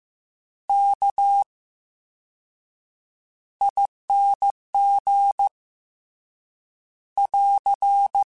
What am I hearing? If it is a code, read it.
Morse code, heard as a keyed tone, about 15 wpm, spaced as for 7 wpm, 800 Hz. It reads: KING <AR>
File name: Special INGs-mp3s / Special INGs-mp3s / VerySlowToFaster Ing @13